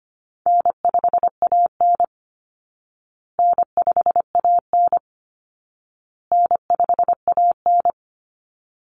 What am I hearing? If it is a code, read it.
Morse code, heard as a keyed tone, about 25 wpm, 700 Hz. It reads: N5AN N5AN N5AN
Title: Easy Call Signs Repeated 3x - 25wpm